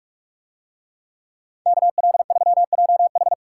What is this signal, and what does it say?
Morse code, heard as a keyed tone, about 43 wpm, 700 Hz. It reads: KL3JH